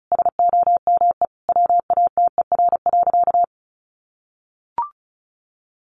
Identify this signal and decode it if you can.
Morse code, heard as a keyed tone, about 34 wpm, 700 Hz. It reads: SOME WATER.
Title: Sentences from Top 1500 Words 35wpm